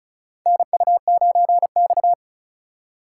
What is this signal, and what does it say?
Morse code, heard as a keyed tone, about 34 wpm, 700 Hz. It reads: NU9X
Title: USA Call Signs - 4x Character Spacing - 35wpm